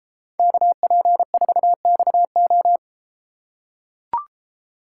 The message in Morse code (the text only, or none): KP4XO